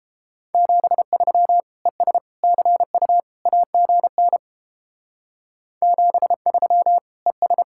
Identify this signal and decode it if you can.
Morse code, heard as a keyed tone, about 29 wpm, 700 Hz. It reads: 73 ES CU AGN 73 ES